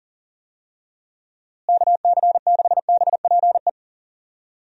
Morse code, heard as a keyed tone, about 40 wpm, 700 Hz. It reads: KC6BPE
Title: USA Call Signs - 6 Characters - Repeated 2x - 40wpm